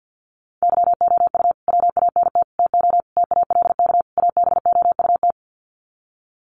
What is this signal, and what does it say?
Morse code, heard as a keyed tone, about 50 wpm, 700 Hz. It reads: YOU WANT TO TALK ABOUT
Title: Sets of 5 Words Encore 50wpm